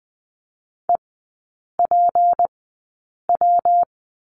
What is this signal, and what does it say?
Morse code, heard as a keyed tone, about 20 wpm, spaced as for 8 wpm, 700 Hz. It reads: EPW
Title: 152 Lesson 25e - Letter p - Three Characters 20wpm